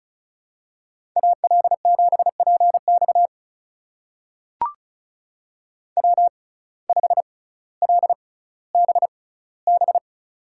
Morse code, heard as a keyed tone, about 36 wpm, spaced as for 18 wpm, 700 Hz. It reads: AL7PX W5LBB
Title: USA Call Signs - 5 Characters - 6x Character Spacing - 35wpm